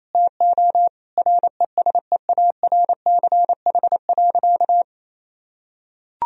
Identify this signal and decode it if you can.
Morse code, heard as a keyed tone, about 28 wpm, 700 Hz. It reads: TO RESEARCH.
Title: Sentences from Top 1000 Words - Review Part 4 - 28wpm